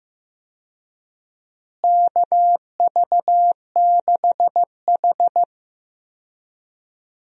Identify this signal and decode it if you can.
Morse code, heard as a keyed tone, about 15 wpm, 700 Hz. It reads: KV6H
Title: USA Call Signs - 4 Characters - Repeated 3x - 15wpm